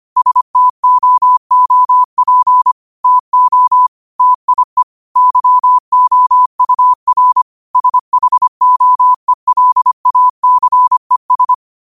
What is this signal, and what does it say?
Morse code, heard as a keyed tone, about 25 wpm, 1000 Hz. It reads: STOOP TO TIE YOUR SHOELACES
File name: qotdfiles / 25wpm